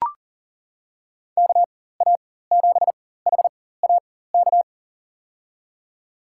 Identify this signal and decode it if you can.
Morse code, heard as a keyed tone, about 40 wpm, spaced as for 19 wpm, 700 Hz. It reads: KA7HAK
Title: USA Call Signs - 6 Characters - 4x Character Spacing - 40wpm